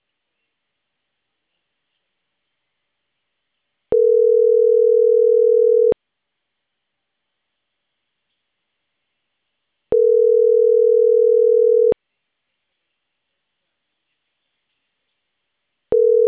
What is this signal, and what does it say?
Ringback tone (North America), 440 + 480 Hz.
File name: CENTRAL CAEA / out agcaea01